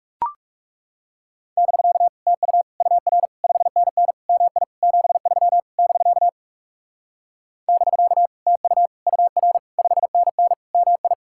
Morse code, heard as a keyed tone, about 38 wpm, 700 Hz. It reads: <BK> TU UR 5NN MI 73 <BK> <BK> TU UR 5NN MI